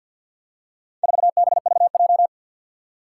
Morse code, heard as a keyed tone, about 51 wpm, 700 Hz. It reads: 4BVJ
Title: Four Challenging Letter-Number Combinations 50wpm